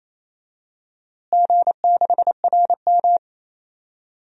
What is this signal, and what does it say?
Morse code, heard as a keyed tone, about 28 wpm, 700 Hz. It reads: G6RM
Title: DXCC-100 Call Signs - 4 Characters - Repeated 3x - 28wpm